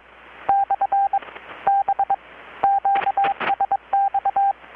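Morse code, heard as a keyed tone, about 23 wpm, 790 Hz, with radio noise. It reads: / B QSX
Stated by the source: Morse from KPH's Night of Nights XX maritime radiotelegraphy event, received in Chiapas, Mexico about 3,800 km away on a Kenwood R-600 with a grounded 10 m long-wire antenna.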